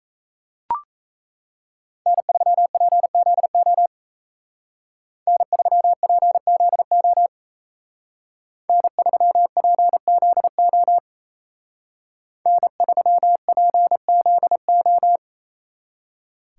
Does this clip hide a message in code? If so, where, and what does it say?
Yes: Morse code from 2.1 to 15.2 s, heard as a keyed tone, about 36 wpm, 700 Hz. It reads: N3PZO N3PZO N3PZO N3PZO